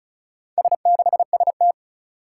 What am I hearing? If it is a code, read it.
Morse code, heard as a keyed tone, about 36 wpm, 700 Hz. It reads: S6ST
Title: Four Challenging Letter-Number Combinations 35wpm